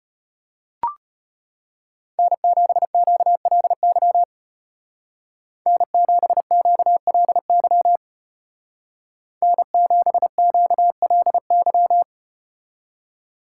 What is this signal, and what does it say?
Morse code, heard as a keyed tone, about 33 wpm, 700 Hz. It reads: N7QLY N7QLY N7QLY